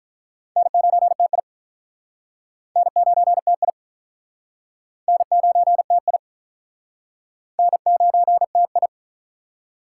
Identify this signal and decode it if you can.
Morse code, heard as a keyed tone, about 45 wpm, 700 Hz. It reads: N9TI N9TI N9TI N9TI